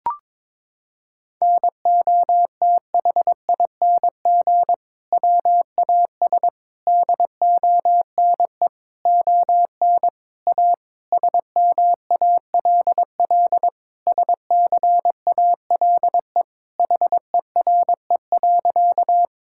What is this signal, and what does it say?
Morse code, heard as a keyed tone, about 22 wpm, 700 Hz. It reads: NOTHING WAS DONE ON A SMALL SCALE HERE.
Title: Sentences from Top 1000 Words - Review Part 5 - 22wpm